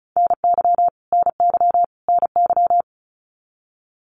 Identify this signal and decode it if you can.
Morse code, heard as a keyed tone, about 34 wpm, 700 Hz. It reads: NY NY NY
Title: US States Repeated 3x 35wpm